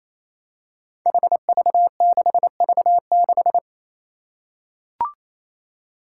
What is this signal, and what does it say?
Morse code, heard as a keyed tone, about 28 wpm, 700 Hz. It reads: HV6V6
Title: Five Challenging Letter-Number Combinations 28wpm